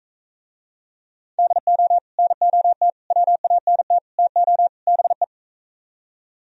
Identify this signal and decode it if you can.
Morse code, heard as a keyed tone, about 41 wpm, 700 Hz. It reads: DO NOT WANT TO BE